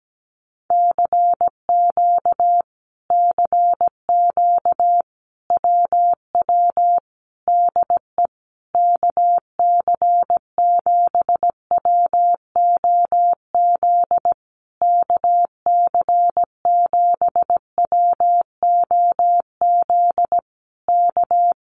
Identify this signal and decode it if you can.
Morse code, heard as a keyed tone, about 17 wpm, 700 Hz. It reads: CQ CQ WW DE KC7WOZ KC7WOZ K
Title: Common QSO Phrases Encore 17wpm